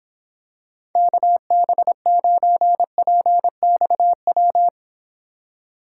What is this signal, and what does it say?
Morse code, heard as a keyed tone, about 26 wpm, 700 Hz. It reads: KB9PXW